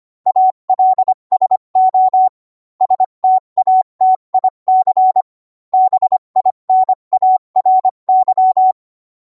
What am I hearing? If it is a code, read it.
Morse code, heard as a keyed tone, about 25 wpm, 750 Hz. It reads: ALSO STATIC BINARY